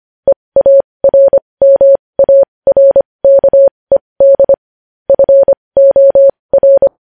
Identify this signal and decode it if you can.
Morse code, heard as a keyed tone, about 25 wpm, spaced as for 20 wpm, 550 Hz. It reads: EARMARKED FOR
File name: podcast / News headlines-1x-25x20